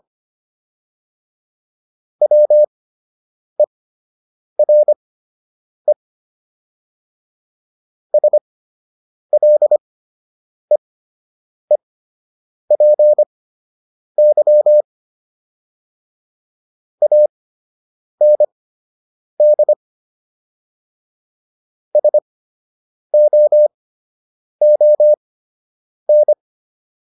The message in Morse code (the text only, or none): WERE SLEEPY AND SOON